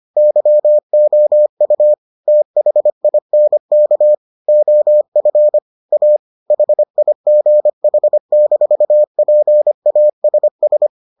Morse code, heard as a keyed tone, about 25 wpm, 600 Hz. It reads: YOU THINK OF A HIGH-PASS